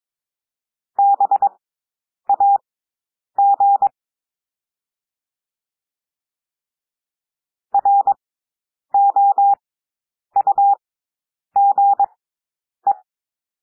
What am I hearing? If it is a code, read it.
Morse code, heard as a keyed tone, about 22 wpm, spaced as for 9 wpm, 800 Hz. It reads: BAG ROUGE
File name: cwmp3 / W2 SOUND WORDS 22 wpm